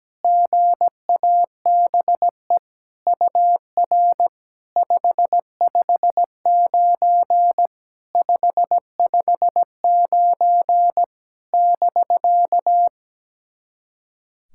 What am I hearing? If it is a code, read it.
Morse code, heard as a keyed tone, about 17 wpm, 700 Hz. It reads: GABE UR 559 559 <BK>
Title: SOTA QSO Practice 17wpm